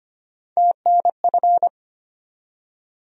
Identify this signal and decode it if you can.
Morse code, heard as a keyed tone, about 25 wpm, 700 Hz. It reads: TNF
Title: USA Call Sign Postfixes - 3 Letters 25wpm